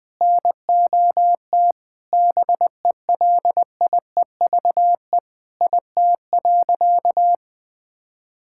Morse code, heard as a keyed tone, about 21 wpm, 700 Hz. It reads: NOT BELIEVE IT.
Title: Sentences from Top 1000 Words - Review Part 6 - 20wpm